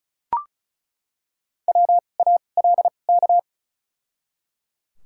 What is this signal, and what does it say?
Morse code, heard as a keyed tone, about 36 wpm, spaced as for 25 wpm, 700 Hz. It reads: WALK
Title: 2x Character Spacing - Top 300 Words - 35wpm